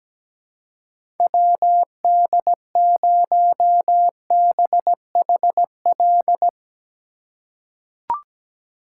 Morse code, heard as a keyed tone, about 17 wpm, 700 Hz. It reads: WD0BHL
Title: USA Call Signs - 6 Characters - 3x Character Spacing - 17wpm